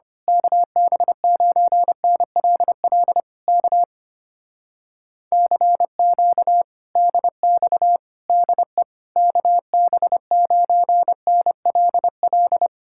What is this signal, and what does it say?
Morse code, heard as a keyed tone, about 24 wpm, 700 Hz. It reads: KB9NLL K CQ DX DE KB9NLL
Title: Speed-Racing Common QSO Phrases - 25wpm